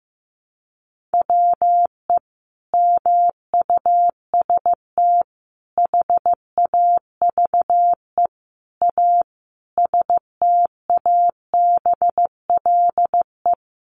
Morse code, heard as a keyed tone, about 15 wpm, 700 Hz. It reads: WE MUST HAVE A STABLE
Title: Sentences from Top 1500 Words 15wpm